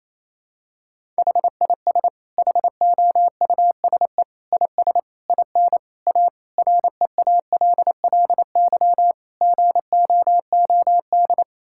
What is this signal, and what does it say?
Morse code, heard as a keyed tone, about 28 wpm, 700 Hz. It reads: HIS HOUSE IS IN A REALLY GOOD
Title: Sentences from Top 1000 Words - Review Part 6 - 28wpm